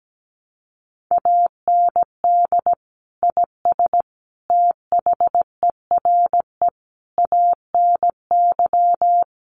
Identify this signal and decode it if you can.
Morse code, heard as a keyed tone, about 17 wpm, 700 Hz. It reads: AND IS THERE ANY
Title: Sentences from Top 1300 Words 17wpm